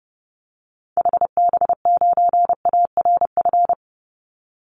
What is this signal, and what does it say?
Morse code, heard as a keyed tone, about 30 wpm, 700 Hz. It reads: HB9ARF
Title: Call Signs - 30wpm